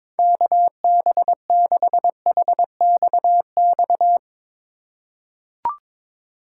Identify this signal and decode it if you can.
Morse code, heard as a keyed tone, about 22 wpm, 700 Hz. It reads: KB6HXX
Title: USA Call Signs - 6 Characters - 4x Character Spacing - 22wpm